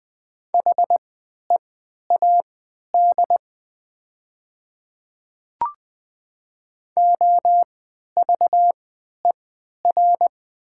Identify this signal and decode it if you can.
Morse code, heard as a keyed tone, about 20 wpm, spaced as for 11 wpm, 700 Hz. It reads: HEAD OVER
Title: 3x Character Spacing - Top 500 Words - 20wpm